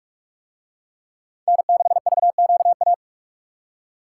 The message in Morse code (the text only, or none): N6VQA